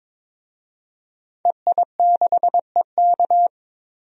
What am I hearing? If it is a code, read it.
Morse code, heard as a keyed tone, about 22 wpm, 700 Hz. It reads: EI6EK
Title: DXCC-100 Call Signs - 5 Characters - 22wpm